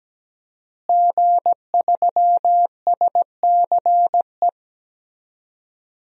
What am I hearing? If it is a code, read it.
Morse code, heard as a keyed tone, about 17 wpm, 700 Hz. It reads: G3SCE
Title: DXCC-100 Call Signs - 5 Characters - Repeated 2x - 17wpm